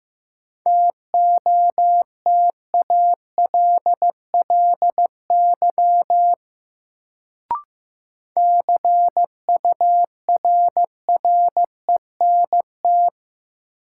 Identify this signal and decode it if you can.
Morse code, heard as a keyed tone, about 15 wpm, 700 Hz. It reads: TOTALLY CURRENT
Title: Sentences from Top 1600 Words - New Word Drill Alt - 15wpm